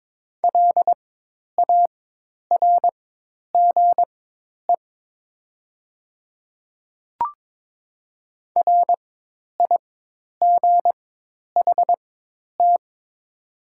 Morse code, heard as a keyed tone, about 22 wpm, spaced as for 10 wpm, 700 Hz. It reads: LARGE RIGHT
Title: 4x Character Spacing - Top 200 Words - 22wpm